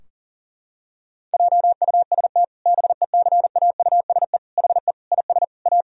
Morse code, heard as a keyed tone, about 40 wpm, 700 Hz. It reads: JUST BECAUSE HE IS A
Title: Sentences from Top 1000 Words - Review Part 3 - 40wpm